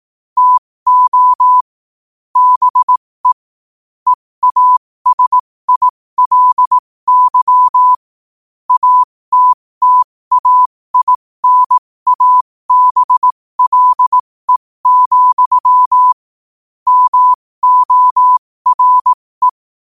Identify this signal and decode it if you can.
Morse code, heard as a keyed tone, about 18 wpm, spaced as for 15 wpm, 1000 Hz. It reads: TO BE EASILY ATTAINABLE, MORE
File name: qotdfiles / 15wpm